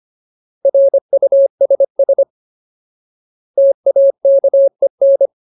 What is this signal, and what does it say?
Morse code, heard as a keyed tone, about 25 wpm, 550 Hz. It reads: RUSS TAKEN
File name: Ultimate Word List / 25WPM Ultimate